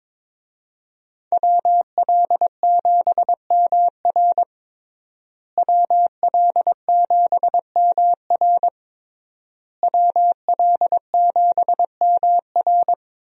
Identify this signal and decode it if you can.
Morse code, heard as a keyed tone, about 22 wpm, 700 Hz. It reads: WL7MR WL7MR WL7MR
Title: USA Call Signs - 5 Characters - Repeated 3x - 22wpm